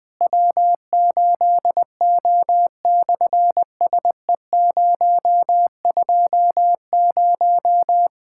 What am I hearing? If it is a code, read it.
Morse code, heard as a keyed tone, about 20 wpm, 700 Hz. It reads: W8O/SE020
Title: SOTA QSO Practice 20wpm